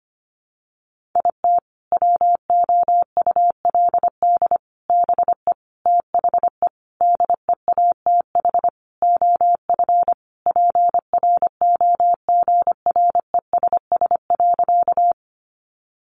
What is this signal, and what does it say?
Morse code, heard as a keyed tone, about 25 wpm, 700 Hz. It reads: IT WOULD BE THE DEATH OF PROGRESS.
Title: Sentences from Top 1400 Words 25wpm